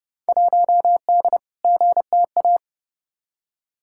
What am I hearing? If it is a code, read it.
Morse code, heard as a keyed tone, about 29 wpm, 700 Hz. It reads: 1D GTA
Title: Field Day Exchanges - 30wpm